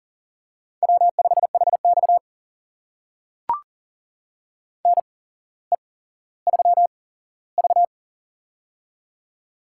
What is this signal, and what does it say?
Morse code, heard as a keyed tone, about 40 wpm, spaced as for 17 wpm, 700 Hz. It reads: W5HX NE3V
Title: USA Call Signs - 4 Characters - 8x Character Spacing - 40wpm